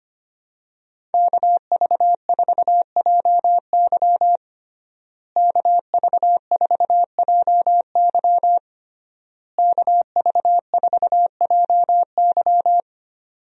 Morse code, heard as a keyed tone, about 25 wpm, 700 Hz. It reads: KV4JY KV4JY KV4JY
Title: USA Call Signs - 5 Characters - Repeated 3x - 25wpm